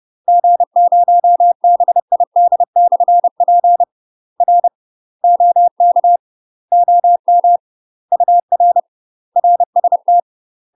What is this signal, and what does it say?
Morse code, heard as a keyed tone, about 30 wpm, 700 Hz. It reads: G0BID/P R OK OM UR RST
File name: UK MASTER QSOs TRAINING / 30WPM / UK MASTER QSOs 0036